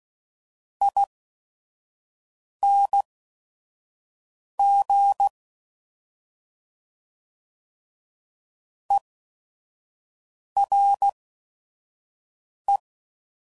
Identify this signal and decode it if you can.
Morse code, heard as a keyed tone, about 16 wpm, spaced as for 5 wpm, 800 Hz. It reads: ING ERE
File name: Suffixes / General / ERE INGs